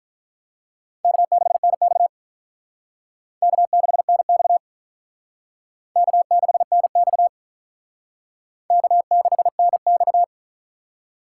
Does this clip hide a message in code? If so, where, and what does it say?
Yes: Morse code from 1.0 to 10.2 s, heard as a keyed tone, about 45 wpm, 700 Hz. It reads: K6NX K6NX K6NX K6NX